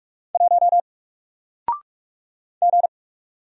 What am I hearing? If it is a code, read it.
Morse code, heard as a keyed tone, about 44 wpm, 700 Hz. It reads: 1 G
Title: Single Character (Letter:Number:Punctuation:Prosign) Alt 45wpm